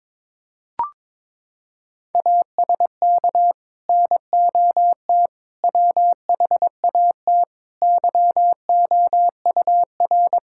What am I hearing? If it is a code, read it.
Morse code, heard as a keyed tone, about 21 wpm, 700 Hz. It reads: ASK NOT WHAT YOUR
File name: Sets of 9 Words - 22wpm